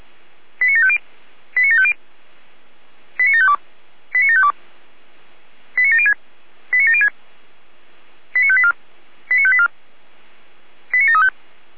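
Five-tone selective call (ZVEI), heard as quick runs of five tones, tones of about 70 ms. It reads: 89750 89750 89752 89752 89706 89706 89584 89584 89735
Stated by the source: ZVEI five-tone call 89531 received by SDR.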